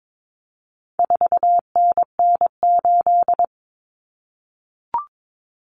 Morse code, heard as a keyed tone, about 22 wpm, 700 Hz. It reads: 4NN8
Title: Common Cut Numbers - 4-Digit 22wpm